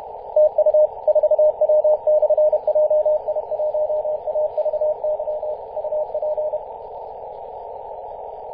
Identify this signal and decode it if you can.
Morse code, heard as a keyed tone, about 31 wpm, 610 Hz, with radio noise. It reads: TU 4W/JI1AVY UP
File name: 2016/Apr/08 0544z 4W/JI1AVY 21.011MHz CW